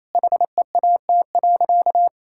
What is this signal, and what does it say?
Morse code, heard as a keyed tone, about 28 wpm, 700 Hz. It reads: HEAT.
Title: Sentences from Top 1000 Words - Review Part 6 - 28wpm